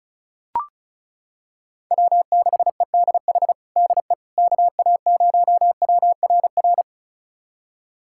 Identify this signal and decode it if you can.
Morse code, heard as a keyed tone, about 36 wpm, 700 Hz. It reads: W6EDH DE KA0WRR
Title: USA Callsign DE Callsign 35wpm